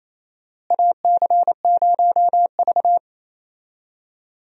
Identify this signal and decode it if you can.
Morse code, heard as a keyed tone, about 28 wpm, 700 Hz. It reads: AC0V
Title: USA Call Signs - 4 Characters - Repeated 3x - 28wpm